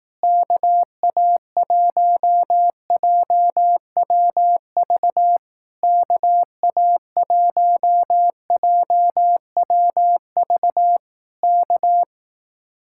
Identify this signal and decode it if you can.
Morse code, heard as a keyed tone, about 18 wpm, 700 Hz. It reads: KA1JWV KA1JWV K
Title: Speed-Racing Common QSO Phrases Encore - 15wpm